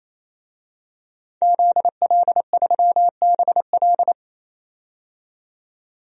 Morse code, heard as a keyed tone, about 28 wpm, 700 Hz. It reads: ZL3BL
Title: DXCC-100 Call Signs - 5 Characters - 28wpm